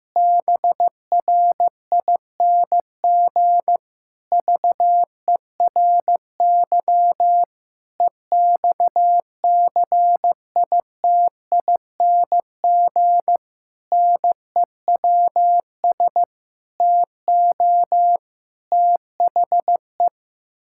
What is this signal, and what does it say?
Morse code, heard as a keyed tone, about 15 wpm, 700 Hz. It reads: BRING VERY EXCITING NEWS TO THE
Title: Pangram Challenge v2 - 15wpm